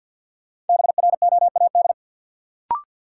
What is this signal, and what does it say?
Morse code, heard as a keyed tone, about 52 wpm, 700 Hz. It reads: BROAD
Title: Sentences from Top 1000 Words - New Word Drill - 50wpm